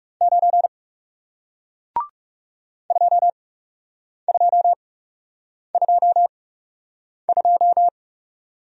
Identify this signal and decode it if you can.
Morse code, heard as a keyed tone, about 37 wpm, spaced as for 8 wpm, 700 Hz. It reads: 9 2222
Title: Speed-Racing Single Letter-Number 30wpm